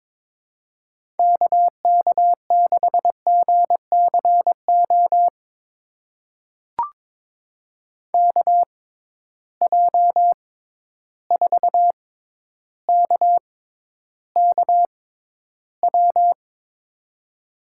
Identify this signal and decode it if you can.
Morse code, heard as a keyed tone, about 22 wpm, spaced as for 11 wpm, 700 Hz. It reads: KK6GCO KJ4KKW
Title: USA Call Signs - 6 Characters - 6x Character Spacing - 22wpm